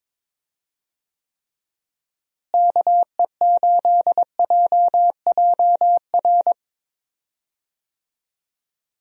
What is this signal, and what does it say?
Morse code, heard as a keyed tone, about 22 wpm, 700 Hz. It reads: KE8JJR
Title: USA Call Signs - 6 Characters - Repeated 3x - 22wpm